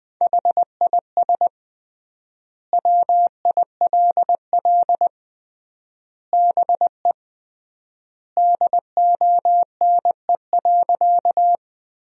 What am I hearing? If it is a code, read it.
Morse code, heard as a keyed tone, about 20 wpm, 700 Hz. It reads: HIS WILL BE DONE.
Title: 3x Word-Spacing - Sentences from Top 500 Words - 20wpm